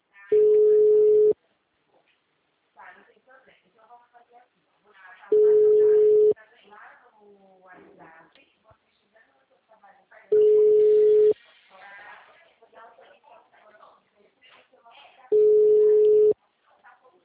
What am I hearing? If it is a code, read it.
Ringback tone (Europe, 425 Hz), with speech.